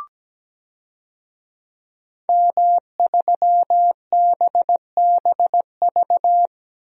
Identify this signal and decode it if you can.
Morse code, heard as a keyed tone, about 17 wpm, 700 Hz. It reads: M3BBV
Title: DXCC-100 Call Signs - 5 Characters - 17wpm